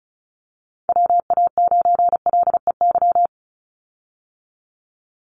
Morse code, heard as a keyed tone, about 34 wpm, 700 Hz. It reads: WA9LEY
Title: Call Signs Repeated 2x - 35wpm